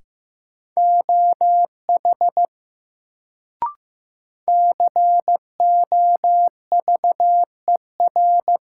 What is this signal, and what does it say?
Morse code, heard as a keyed tone, about 15 wpm, 700 Hz. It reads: OH COVER
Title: Sentences from Top 500 Words - New Word Drill - 15wpm